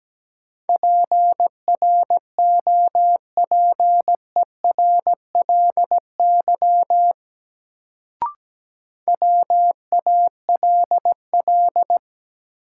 Morse code, heard as a keyed tone, about 17 wpm, 700 Hz. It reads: PROPERLY WALL
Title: Sentences from Top 1100 Words - New Word Drill Alt - 17wpm